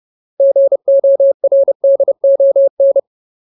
Morse code, heard as a keyed tone, about 30 wpm, 550 Hz. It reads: GORDON
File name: Ultimate Word List / 30WPM Ultimate